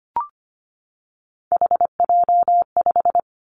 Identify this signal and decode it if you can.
Morse code, heard as a keyed tone, about 25 wpm, 700 Hz. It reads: HJ5
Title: Callsign Prefixes for Top 100 DXCC - 25wpm